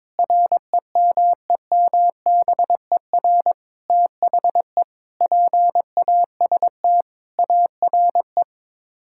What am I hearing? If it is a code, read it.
Morse code, heard as a keyed tone, about 22 wpm, 700 Hz. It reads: REMEMBER THE PAST ARE
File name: practice / 22wpm / Sets of 8 Words - 22wpm